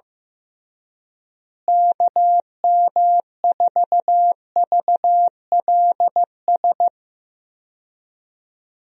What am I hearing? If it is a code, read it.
Morse code, heard as a keyed tone, about 15 wpm, 700 Hz. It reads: KM4VLS
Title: USA Call Signs - 6 Characters - Repeated 3x - 15wpm